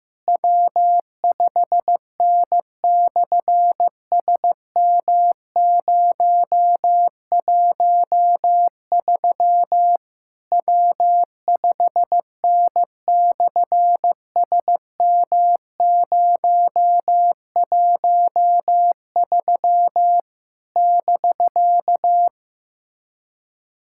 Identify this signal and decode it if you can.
Morse code, heard as a keyed tone, about 15 wpm, 700 Hz. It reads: W5N/SM013 W5N/SM013 <BK>
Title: SOTA QSO Practice 15wpm